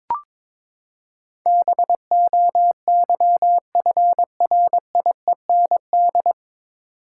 Morse code, heard as a keyed tone, about 22 wpm, 700 Hz. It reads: BOYFRIEND
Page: Morse Code Ninja practice 22wpm